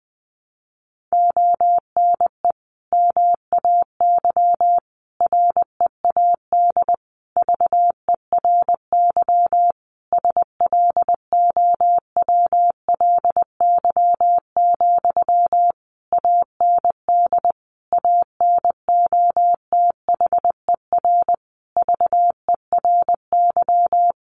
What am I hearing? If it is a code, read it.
Morse code, heard as a keyed tone, about 20 wpm, 700 Hz. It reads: ONE MAY READ VERY SLOWLY, AND ANOTHER VERY
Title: Sentences from Top 1100 Words 20wpm